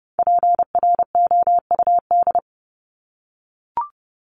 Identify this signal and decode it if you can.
Morse code, heard as a keyed tone, about 30 wpm, 700 Hz. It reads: PROUD
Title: Top 1501-1750 Words 30wpm